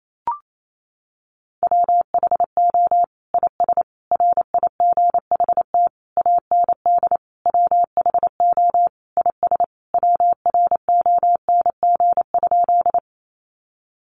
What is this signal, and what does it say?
Morse code, heard as a keyed tone, about 28 wpm, 700 Hz. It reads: WHO IS RIGHT AND WHO IS WRONG?
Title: Sentences from Top 900 Words 28wpm